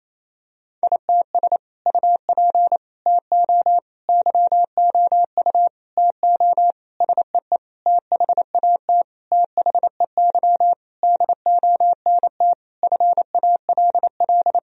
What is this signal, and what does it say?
Morse code, heard as a keyed tone, about 28 wpm, 700 Hz. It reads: ITS UP TO YOU TO SEE THAT THEY DONT FALL
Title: Sentences from Top 1000 Words - Review Part 2 - 28wpm